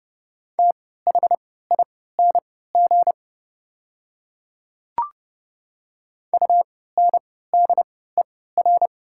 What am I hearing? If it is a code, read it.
Morse code, heard as a keyed tone, about 30 wpm, spaced as for 17 wpm, 700 Hz. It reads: THING UNDER
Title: 3x Character Spacing - Top 300 Words - 30wpm